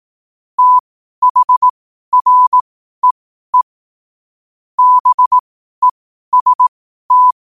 Morse code, heard as a keyed tone, about 18 wpm, spaced as for 12 wpm, 1000 Hz. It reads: THREE BEST
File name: qotdfiles / 12wpm